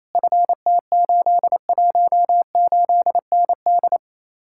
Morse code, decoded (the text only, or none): FT818ND